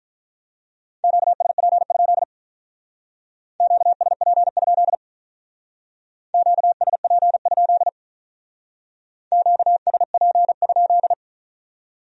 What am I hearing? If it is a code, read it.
Morse code, heard as a keyed tone, about 44 wpm, 700 Hz. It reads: QSP? QSP? QSP? QSP?